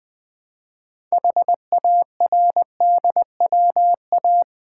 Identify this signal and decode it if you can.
Morse code, heard as a keyed tone, about 20 wpm, 700 Hz. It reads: HARDWA
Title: Build A Word - 3 Letters At A Time - 8-Letter Words - 20wpm